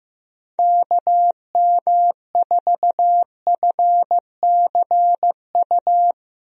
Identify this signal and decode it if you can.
Morse code, heard as a keyed tone, about 15 wpm, 700 Hz. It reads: KM4FCU
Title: USA Call Signs - 6 Characters - 15wpm